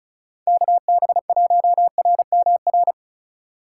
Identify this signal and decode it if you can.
Morse code, heard as a keyed tone, about 44 wpm, 700 Hz. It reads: KB1RMR